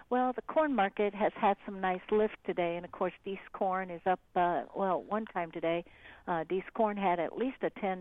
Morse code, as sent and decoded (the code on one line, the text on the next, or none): none